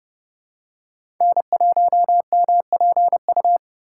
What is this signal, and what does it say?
Morse code, heard as a keyed tone, about 30 wpm, 700 Hz. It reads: N1MPU
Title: USA Call Signs - 5 Characters - 4x Character Spacing - 30wpm